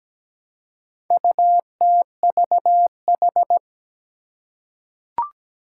Morse code, heard as a keyed tone, about 17 wpm, 700 Hz. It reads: UTVH
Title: Four Challenging Letter-Number Combinations 17wpm